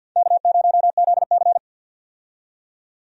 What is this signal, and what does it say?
Morse code, heard as a keyed tone, about 46 wpm, 700 Hz. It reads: K0ZC
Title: USA Call Signs - 4 Characters - 50wpm